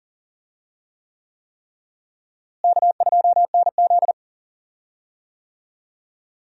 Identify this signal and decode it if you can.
Morse code, heard as a keyed tone, about 40 wpm, 700 Hz. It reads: K2NZ
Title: USA Call Signs - 4 Characters - Repeated 3x - 40wpm